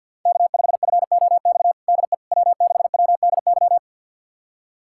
Morse code, heard as a keyed tone, about 52 wpm, 700 Hz. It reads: K5FOX DE W6WDY
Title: USA Callsign DE Callsign - 5 Characters - 50wpm